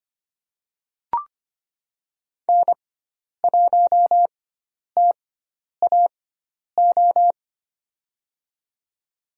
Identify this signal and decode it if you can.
Morse code, heard as a keyed tone, about 25 wpm, spaced as for 10 wpm, 700 Hz. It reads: N1TAO
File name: Lesson 16g - Letter c - Call Signs 25wpm